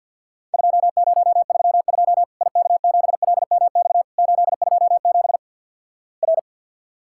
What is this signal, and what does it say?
Morse code, heard as a keyed tone, about 54 wpm, 700 Hz. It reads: 2032 IK7LMX 827 R
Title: Worked All Europe DX Contest - QTC Exchange with Repeats - 50wpm